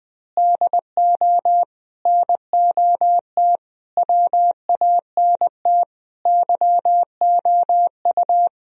Morse code, heard as a keyed tone, about 20 wpm, 700 Hz. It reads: DO NOT WANT YOU
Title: Speed-Racing - Sets of 5 Words 17wpm